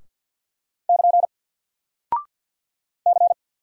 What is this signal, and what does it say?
Morse code, heard as a keyed tone, about 48 wpm, 700 Hz. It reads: <KN> C
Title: Single Character (Letter:Number:Punctuation:Prosign) Alt 50wpm